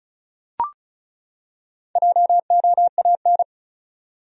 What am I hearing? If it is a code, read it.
Morse code, heard as a keyed tone, about 34 wpm, 700 Hz. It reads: JOAN
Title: USA Common Names - 35wpm